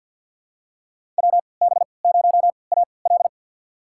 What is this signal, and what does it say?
Morse code, heard as a keyed tone, about 52 wpm, spaced as for 29 wpm, 700 Hz. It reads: WB0AL
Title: USA Call Signs - 3x Character Spacing - 50wpm